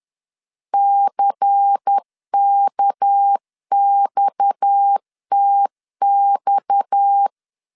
Morse code, heard as a keyed tone, about 10 wpm, 800 Hz. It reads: CKXTX
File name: LEZIONI / 6 PKX0 / 6 PKX 10wpm 2426